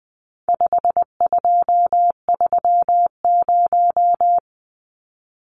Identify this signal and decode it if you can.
Morse code, heard as a keyed tone, about 20 wpm, 700 Hz. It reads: 5230
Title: Numbers - 4-Digit 20wpm